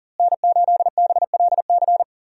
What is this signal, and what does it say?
Morse code, heard as a keyed tone, about 37 wpm, 700 Hz. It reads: N8BLC
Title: USA Call Signs - 5 Characters - Repeated 2x - 40wpm